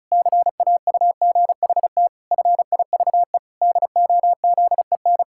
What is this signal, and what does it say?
Morse code, heard as a keyed tone, about 36 wpm, 700 Hz. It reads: CAUGHT FIVE DOZEN
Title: Pangram Challenge v2 - 35wpm